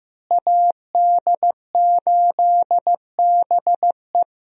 Morse code, heard as a keyed tone, about 15 wpm, 700 Hz. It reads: AD8BE
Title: USA Call Signs - 5 Characters - Repeated 2x - 15wpm